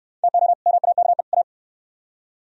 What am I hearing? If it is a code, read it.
Morse code, heard as a keyed tone, about 62 wpm, 700 Hz. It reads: TO MAKE A